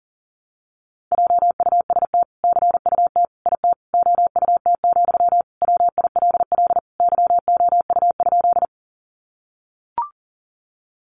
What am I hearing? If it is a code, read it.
Morse code, heard as a keyed tone, about 40 wpm, 700 Hz. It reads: JUST CUT IT OUT, WILL YOU?
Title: Sentences from Top 300 Words 40wpm